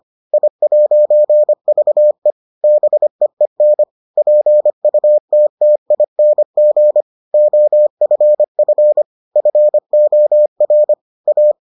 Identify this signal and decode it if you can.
Morse code, heard as a keyed tone, about 25 wpm, 600 Hz. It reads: I'VE BEEN PUTTING OFF FOR A